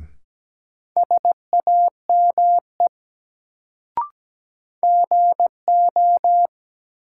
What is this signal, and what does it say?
Morse code, heard as a keyed tone, about 17 wpm, 700 Hz. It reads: SAME GO
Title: Sets of 2 Words - New Word Drill - 17wpm